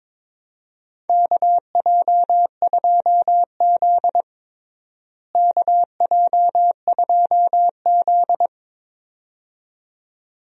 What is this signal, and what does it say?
Morse code, heard as a keyed tone, about 22 wpm, 700 Hz. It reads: KJ2Z KJ2Z
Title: USA Call Signs - 4 Characters - Repeated 2x - 22wpm